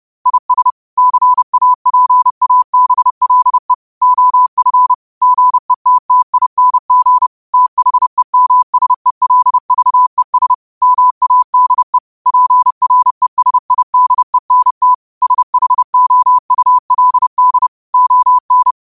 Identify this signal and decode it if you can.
Morse code, heard as a keyed tone, about 30 wpm, 1000 Hz. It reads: IS CAPABLE OF GETTING THEMSELVES MADE PRESIDENT SHOULD ON